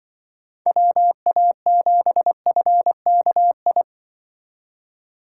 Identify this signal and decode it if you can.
Morse code, heard as a keyed tone, about 24 wpm, 700 Hz. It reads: WA7FKI
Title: Speed Racing - USA 2x3 Call Signs 20wpm